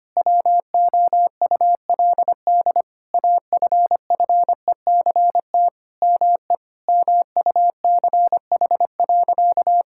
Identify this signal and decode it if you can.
Morse code, heard as a keyed tone, about 25 wpm, 700 Hz. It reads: WOULD AFFECT ME MUCH.
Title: Sentences from Top 1000 Words - Review Part 6 - 25wpm